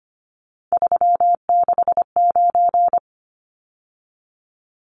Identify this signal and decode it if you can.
Morse code, heard as a keyed tone, about 25 wpm, 700 Hz. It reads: 369
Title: Numbers - 3-Digit 25wpm